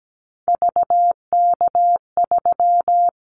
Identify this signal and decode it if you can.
Morse code, heard as a keyed tone, about 17 wpm, 700 Hz. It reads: VK3
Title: Callsign Prefixes for Top 50 DXCC - 17wpm